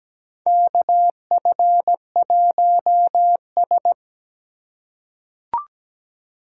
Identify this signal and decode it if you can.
Morse code, heard as a keyed tone, about 17 wpm, 700 Hz. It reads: KF1S
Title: USA Call Signs - 4 Characters - 2x Character Spacing - 17wpm